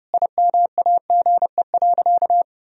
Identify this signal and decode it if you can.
Morse code, heard as a keyed tone, about 30 wpm, 700 Hz. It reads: IMAGE.
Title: Sentences from Top 1000 Words - Review Part 4 - 30wpm